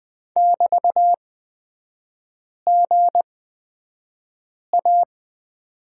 Morse code, heard as a keyed tone, about 20 wpm, spaced as for 5 wpm, 690 Hz. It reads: <BT>GA
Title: 229 Lesson 37e - Letter x - Three Characters 20wpm